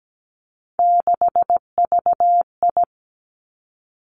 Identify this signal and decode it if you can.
Morse code, heard as a keyed tone, about 17 wpm, 700 Hz. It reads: 6VI
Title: Three Challenging Letter-Number Combinations 17wpm